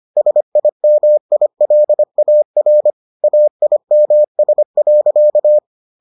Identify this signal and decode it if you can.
Morse code, heard as a keyed tone, about 25 wpm, 600 Hz. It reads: SIMILAR AIMS.